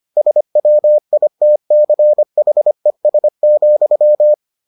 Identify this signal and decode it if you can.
Morse code, heard as a keyed tone, about 25 wpm, 600 Hz. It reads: SWITCHES,